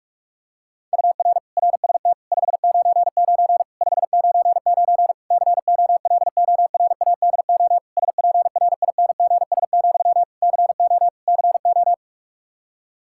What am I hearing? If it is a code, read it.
Morse code, heard as a keyed tone, about 46 wpm, 700 Hz. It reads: UR RST 599 599 COLORADO SPRINGS, CO CO
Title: K1USN SST QSO Practice 45wpm